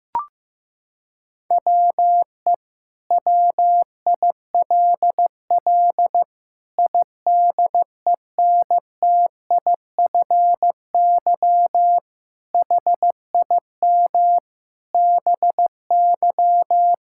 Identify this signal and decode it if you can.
Morse code, heard as a keyed tone, about 15 wpm, 700 Hz. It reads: WE WILL IDENTIFY HIM BY
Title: Sentences from Top 1000 Words - Review Part 6 - 15wpm